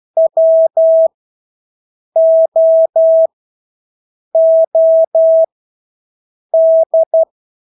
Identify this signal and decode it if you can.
Morse code, heard as a keyed tone, about 12 wpm, spaced as for 6 wpm, 650 Hz. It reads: WOOD
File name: Engelske ord 12-6 wpm